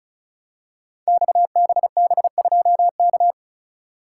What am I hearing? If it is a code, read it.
Morse code, heard as a keyed tone, about 36 wpm, 700 Hz. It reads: XBB2K
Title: Code Groups - 5 Letter:Numbers - 35wpm